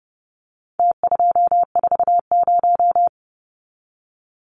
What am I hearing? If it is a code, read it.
Morse code, heard as a keyed tone, about 30 wpm, 700 Hz. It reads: T240
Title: Numbers - Contest Style - 4-Digit 30wpm